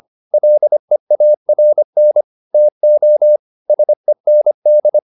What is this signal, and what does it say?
Morse code, heard as a keyed tone, about 25 wpm, 600 Hz. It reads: LEARN TO SEND
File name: SS315 25